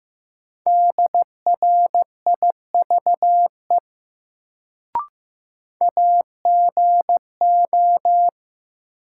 Morse code, heard as a keyed tone, about 15 wpm, 700 Hz. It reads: DRIVE AGO
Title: Sentences from Top 500 Words - New Word Drill - 15wpm